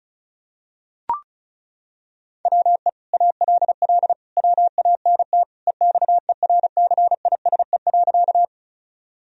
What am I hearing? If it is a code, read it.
Morse code, heard as a keyed tone, about 36 wpm, 700 Hz. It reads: WE ALL WANT EXERCISE.
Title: Sentences from Top 1000 Words - Review Part 4 - 35wpm